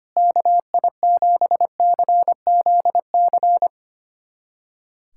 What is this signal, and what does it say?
Morse code, heard as a keyed tone, about 25 wpm, 700 Hz. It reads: KI7CZC
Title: USA Callsign DE Callsign - 6 Characters - 25wpm